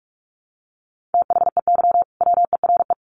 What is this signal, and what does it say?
Morse code, heard as a keyed tone, about 44 wpm, 700 Hz. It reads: THEY WERE
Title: Sentences from Top 1500 Words 45wpm